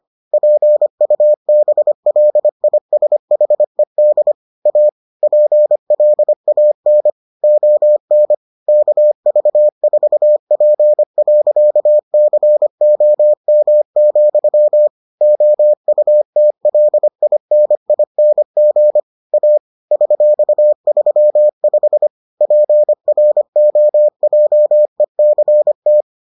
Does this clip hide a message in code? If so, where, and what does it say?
Yes: Morse code from 0.3 to 26.0 s, heard as a keyed tone, about 25 wpm, 600 Hz. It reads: PUBLISHED A PLAN ON KV4P.COM, OUTLINING A $35 PROJECT